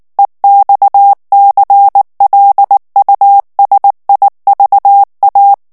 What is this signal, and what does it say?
Morse code, heard as a keyed tone, about 19 wpm, 800 Hz. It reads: EXCLUSIVA
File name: morse mp3 / ESPAGNOL TEXTE EN CLAIR 2 VITESSE 18 MOTS MINUTES